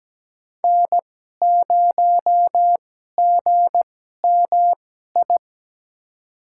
Morse code, heard as a keyed tone, about 17 wpm, spaced as for 12 wpm, 700 Hz. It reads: N0GMI
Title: USA Call Signs - 5 Characters - 2x Character Spacing - 17wpm